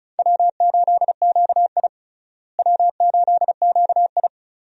Morse code, heard as a keyed tone, about 34 wpm, 700 Hz. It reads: W8QI W8QI
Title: USA Call Signs - 4 Characters - Repeated 3x - 35wpm